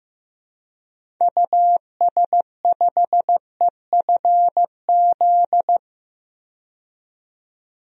Morse code, heard as a keyed tone, about 15 wpm, 700 Hz. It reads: US5EFZ
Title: DXCC-100 Call Signs - 6 Characters - 15wpm